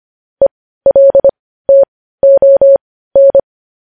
Morse code, heard as a keyed tone, about 25 wpm, spaced as for 15 wpm, 550 Hz. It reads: ELTON
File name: podcast / News headlines-1x-25x15